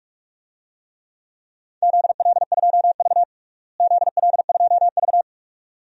Morse code, heard as a keyed tone, about 47 wpm, 700 Hz. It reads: ZL2V ZL2V
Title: DXCC-100 Call Signs - 4 Characters - Repeated 2x - 45wpm